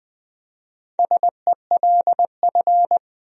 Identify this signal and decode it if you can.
Morse code, heard as a keyed tone, about 21 wpm, 700 Hz. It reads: SELF
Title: Speed-Racing Top 200 Words 17wpm